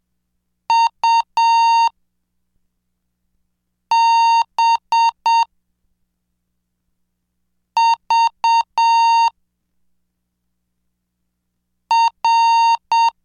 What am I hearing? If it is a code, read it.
Morse code, heard as a keyed tone, about 7 wpm, 920 Hz. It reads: U B V R
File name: cursos / CDestudo / 16 Track 16